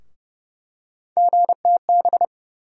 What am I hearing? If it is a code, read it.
Morse code, heard as a keyed tone, about 30 wpm, 700 Hz. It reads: GTB
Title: USA Call Sign Postfixes - 3 Letters 30wpm